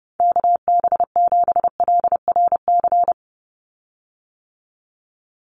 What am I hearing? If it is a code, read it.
Morse code, heard as a keyed tone, about 28 wpm, 700 Hz. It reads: KB7LRC
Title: USA Call Signs 30wpm